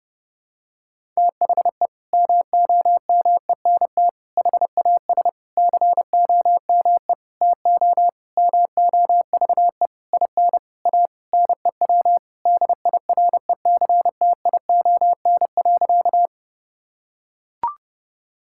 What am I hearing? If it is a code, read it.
Morse code, heard as a keyed tone, about 30 wpm, 700 Hz. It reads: THE MOMENT HAS COME TO MOVE IN A NEW DIRECTION.
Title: Sentences from Top 1000 Words - Review Part 5 - 30wpm